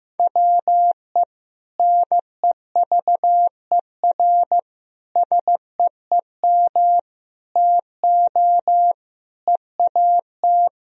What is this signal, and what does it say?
Morse code, heard as a keyed tone, about 15 wpm, 700 Hz. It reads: WE NEVER SEEM TO EAT
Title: Sentences from Top 1000 Words - Review Part 1 - 15wpm